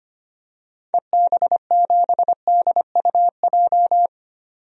Morse code, heard as a keyed tone, about 25 wpm, 690 Hz. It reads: EB7DUJ
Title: DXCC-100 Call Signs - 6 Characters - 25wpm